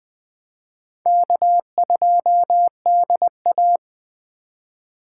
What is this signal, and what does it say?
Morse code, heard as a keyed tone, about 20 wpm, 700 Hz. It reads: K2DA
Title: Speed Racing - USA 1x2 Call Signs 17wpm